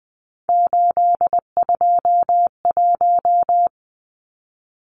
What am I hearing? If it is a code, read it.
Morse code, heard as a keyed tone, about 20 wpm, 700 Hz. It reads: 821
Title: Numbers - 3-Digit 20wpm